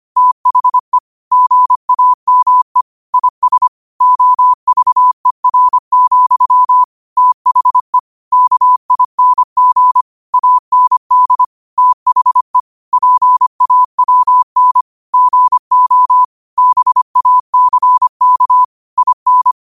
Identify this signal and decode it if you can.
Morse code, heard as a keyed tone, about 25 wpm, 1000 Hz. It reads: THE GAME IS OVER, THE KING AND THE PAWN GO BACK IN
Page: New quotes every day in morse code at 25 Words per minute.